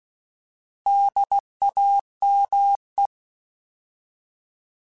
Morse code, heard as a keyed tone, about 16 wpm, 800 Hz. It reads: DAME
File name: Alphabetical / Alphabetical mp3s / Ds @15